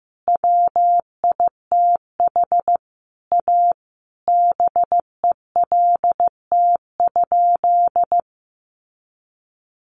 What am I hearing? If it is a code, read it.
Morse code, heard as a keyed tone, about 15 wpm, 700 Hz. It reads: WITH A BELT?
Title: Sentences from Top 1400 Words 15wpm